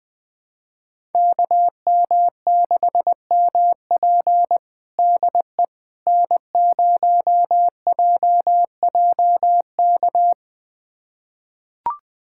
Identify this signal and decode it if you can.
Morse code, heard as a keyed tone, about 21 wpm, 700 Hz. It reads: KM6MP DE N0JJK
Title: USA Callsign DE Callsign - 5 Characters - 20wpm